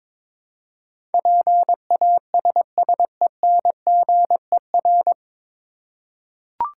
Morse code, heard as a keyed tone, about 22 wpm, 700 Hz. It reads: PASSENGER
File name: 9 Letter Words - First 3 Letters Given - 22wpm